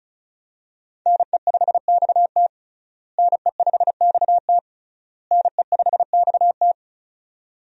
Morse code, heard as a keyed tone, about 35 wpm, 700 Hz. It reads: NE5XT NE5XT NE5XT
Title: USA Call Signs - 5 Characters - Repeated 3x - 35wpm